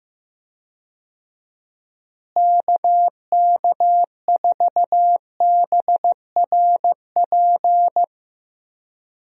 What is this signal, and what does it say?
Morse code, heard as a keyed tone, about 15 wpm, 700 Hz. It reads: KK4BRP
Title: USA Call Signs - 6 Characters - Repeated 2x - 15wpm